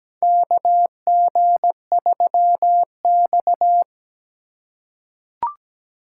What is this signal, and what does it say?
Morse code, heard as a keyed tone, about 17 wpm, 700 Hz. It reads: KG3X
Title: USA Call Signs - 4 Characters - 3x Character Spacing - 17wpm